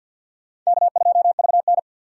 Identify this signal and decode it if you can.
Morse code, heard as a keyed tone, about 48 wpm, 700 Hz. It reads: K2VN
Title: USA Call Signs - 4 Characters - 6x Character Spacing - 50wpm